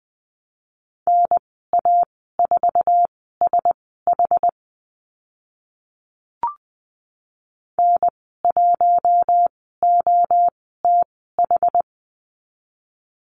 Morse code, heard as a keyed tone, about 20 wpm, spaced as for 14 wpm, 700 Hz. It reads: NA4SH N1OTH
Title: 069 Lesson 12g - Letter d - Call Signs 20wpm